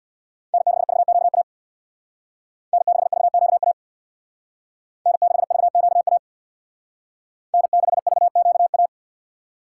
Morse code, heard as a keyed tone, about 61 wpm, 700 Hz. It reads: N6VQA N6VQA N6VQA N6VQA
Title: Speed Racing - USA 1x3 Call Signs 50wpm